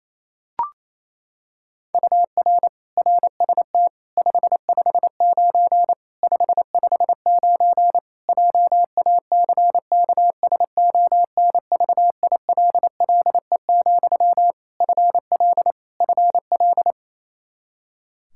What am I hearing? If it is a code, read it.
Morse code, heard as a keyed tone, about 28 wpm, 700 Hz. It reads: UR RST 559 559 JACKSONVILLE, FL FL
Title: Practice QSO - Exchange 1 - Beginning - 28wpm